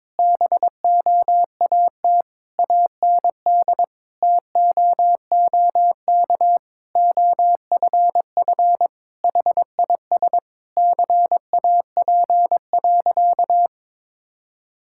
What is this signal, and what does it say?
Morse code, heard as a keyed tone, about 22 wpm, 700 Hz. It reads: BOAT AND TOOK OFF HIS CAP.
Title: Sentences from Top 1600 Words 22wpm